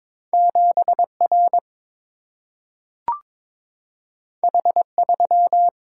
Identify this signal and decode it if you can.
Morse code, heard as a keyed tone, about 22 wpm, 700 Hz. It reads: 7R H3
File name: practice / 22wpm / Two Letter-Number Combinations - 22wpm